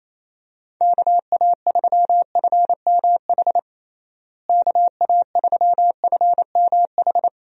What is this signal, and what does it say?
Morse code, heard as a keyed tone, about 28 wpm, 700 Hz. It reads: KA3FMH KA3FMH
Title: USA Call Signs - 6 Characters - Repeated 2x - 28wpm